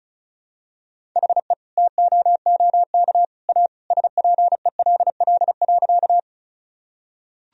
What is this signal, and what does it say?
Morse code, heard as a keyed tone, about 36 wpm, 700 Hz. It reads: HE TOOK A SPELL.
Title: Sentences from Top 1000 Words - Review Part 1 - 35wpm